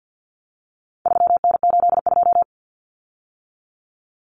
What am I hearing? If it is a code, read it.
Morse code, heard as a keyed tone, about 50 wpm, 700 Hz. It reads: 3N82
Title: Common Cut Numbers - 4-Digit 50wpm